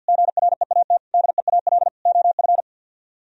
Morse code, heard as a keyed tone, about 64 wpm, 700 Hz. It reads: GREAT DEAL OF